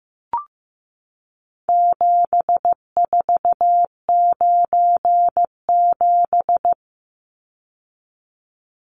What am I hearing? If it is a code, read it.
Morse code, heard as a keyed tone, about 15 wpm, 700 Hz. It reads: 7497
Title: Numbers - 4-Digit 15wpm